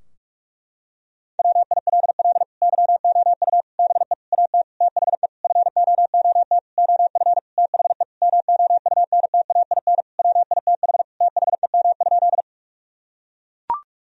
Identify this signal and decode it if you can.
Morse code, heard as a keyed tone, about 41 wpm, 700 Hz. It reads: WILL YOU BE AT THE FOOT OF THE MOUNTAIN WITH THEM?
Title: Sentences from Top 1000 Words - Review Part 3 - 45wpm